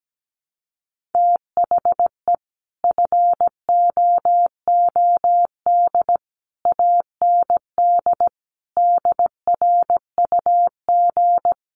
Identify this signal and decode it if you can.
Morse code, heard as a keyed tone, about 17 wpm, 700 Hz. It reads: THE FOOD AND DRUG
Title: Sets of 5 Words 17wpm